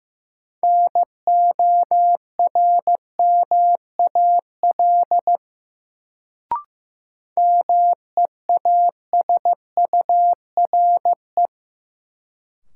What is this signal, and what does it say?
Morse code, heard as a keyed tone, about 15 wpm, 700 Hz. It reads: NORMAL MEASURE
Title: Sentences from Top 1200 Words - New Word Drill Alt - 15wpm